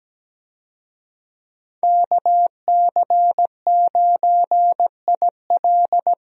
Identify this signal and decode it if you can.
Morse code, heard as a keyed tone, about 17 wpm, 700 Hz. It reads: KC9IL
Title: USA Call Signs - 5 Characters - Repeated 2x - 17wpm